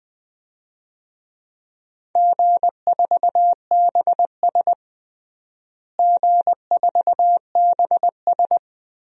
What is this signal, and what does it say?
Morse code, heard as a keyed tone, about 20 wpm, 700 Hz. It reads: G4BS G4BS
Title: DXCC-100 Call Signs - 4 Characters - Repeated 2x - 20wpm